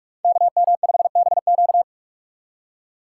Morse code, heard as a keyed tone, about 46 wpm, 700 Hz. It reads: KM5BQ
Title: USA Call Signs - 5 Characters - 2x Character Spacing - 45wpm